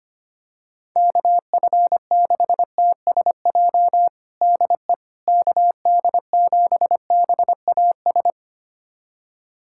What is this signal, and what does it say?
Morse code, heard as a keyed tone, about 25 wpm, 700 Hz. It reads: KF6TSJ DE KD7BAS
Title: USA Callsign DE Callsign - 6 Characters - 25wpm